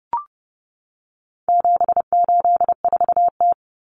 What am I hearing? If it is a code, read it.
Morse code, heard as a keyed tone, about 30 wpm, 700 Hz. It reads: 784T
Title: Common Cut Numbers - 4-Digit 30wpm